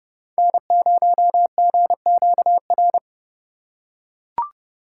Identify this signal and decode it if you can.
Morse code, heard as a keyed tone, about 29 wpm, 700 Hz. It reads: N0GQR